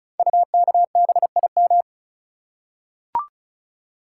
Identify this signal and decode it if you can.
Morse code, heard as a keyed tone, about 36 wpm, 700 Hz. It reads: UKBIM
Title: Code Groups - 5 Letters - 35wpm